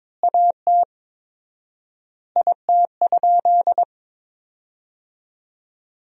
Morse code, heard as a keyed tone, about 23 wpm, 700 Hz. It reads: AT IT?
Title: 4x Word-Spacing - Sentences from Top 100 Words - 22wpm